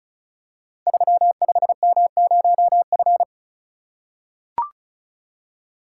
Morse code, heard as a keyed tone, about 36 wpm, 690 Hz. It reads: 35M0F
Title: Code Groups - 5 Letter:Numbers - 35wpm